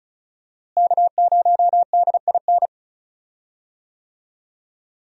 Morse code, heard as a keyed tone, about 36 wpm, 700 Hz. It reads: K0DIN
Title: USA Call Signs - 5 Characters - Repeated 3x - 35wpm